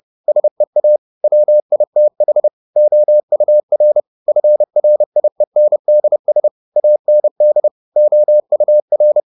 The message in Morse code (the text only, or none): SEA WITH OUR FRIENDS AND OUR